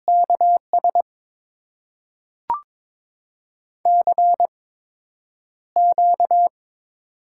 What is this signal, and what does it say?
Morse code, heard as a keyed tone, about 22 wpm, spaced as for 9 wpm, 700 Hz. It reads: KS CQ
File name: Top 90 QSO Elements - 8x Character Spacing - 22wpm